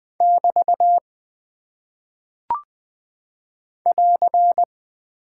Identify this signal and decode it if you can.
Morse code, heard as a keyed tone, about 20 wpm, 700 Hz. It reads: <BT> <AR>
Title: Single Punctuation-Prosign - 20wpm